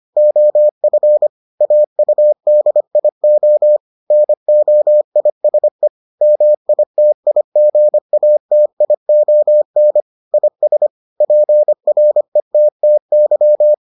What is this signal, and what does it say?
Morse code, heard as a keyed tone, about 25 wpm, 600 Hz. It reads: OF AUDIO NOISE MITIGATION IS PRETTY